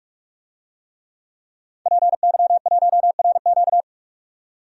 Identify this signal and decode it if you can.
Morse code, heard as a keyed tone, about 46 wpm, 700 Hz. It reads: PY1RQ
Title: DXCC-100 Call Signs - 5 Characters - 45wpm